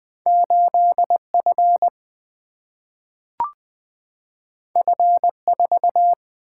Morse code, heard as a keyed tone, about 20 wpm, 700 Hz. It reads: 8F F4
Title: Single-Letter Single-Number 20wpm